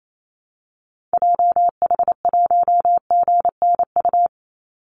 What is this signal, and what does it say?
Morse code, heard as a keyed tone, about 28 wpm, 700 Hz. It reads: JH1GNU
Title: Call Signs - 28wpm